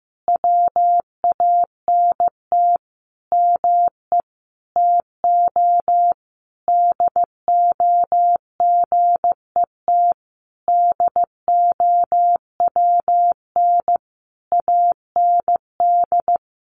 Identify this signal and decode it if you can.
Morse code, heard as a keyed tone, about 15 wpm, 700 Hz. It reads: WANT ME TO DO GET DOWN AND
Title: A Sound Of Thunder 15wpm